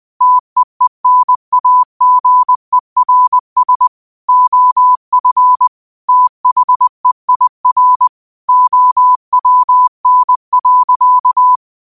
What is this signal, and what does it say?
Morse code, heard as a keyed tone, about 20 wpm, 1000 Hz. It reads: TEENAGERS OF THEIR OWN.